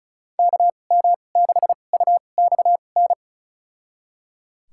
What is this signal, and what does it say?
Morse code, heard as a keyed tone, about 34 wpm, spaced as for 25 wpm, 700 Hz. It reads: KM6UXN
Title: USA Call Signs - 2x Character Spacing - 35wpm